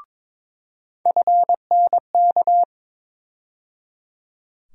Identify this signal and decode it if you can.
Morse code, heard as a keyed tone, about 22 wpm, 700 Hz. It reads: FNK
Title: USA Call Sign Postfixes - 3 Letters 22wpm